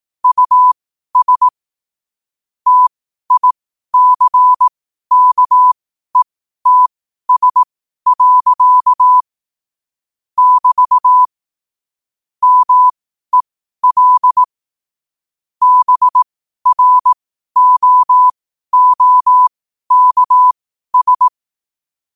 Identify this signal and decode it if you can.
Morse code, heard as a keyed tone, about 18 wpm, spaced as for 12 wpm, 1000 Hz. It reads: US TICKETS. <BT> MEL BROOKS
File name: qotdfiles / 12wpm